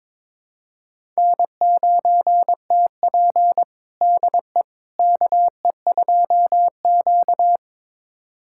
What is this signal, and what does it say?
Morse code, heard as a keyed tone, about 22 wpm, 700 Hz. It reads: N9TP DE KE2Q
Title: USA Callsign DE Callsign - 4 Characters - 22wpm